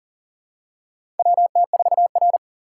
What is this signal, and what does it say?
Morse code, heard as a keyed tone, about 37 wpm, 700 Hz. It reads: WT4R
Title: USA Call Signs - 4 Characters - 3x Character Spacing - 40wpm